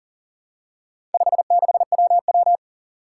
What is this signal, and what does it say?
Morse code, heard as a keyed tone, about 40 wpm, 700 Hz. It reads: 56WW
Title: Four Challenging Letter-Number Combinations 40wpm